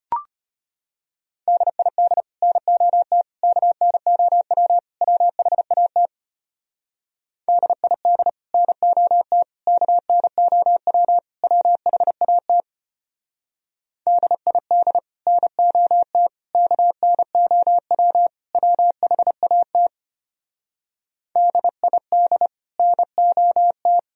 Morse code, heard as a keyed tone, about 30 wpm, 700 Hz. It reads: DID NOT KNOW WHAT DID NOT KNOW WHAT DID NOT KNOW WHAT DID NOT
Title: Speed-Racing - Sets of 4 Words 25wpm